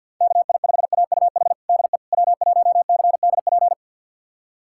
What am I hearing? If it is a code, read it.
Morse code, heard as a keyed tone, about 47 wpm, 700 Hz. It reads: KI5AUH DE W1CDP
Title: USA Callsign DE Callsign 50wpm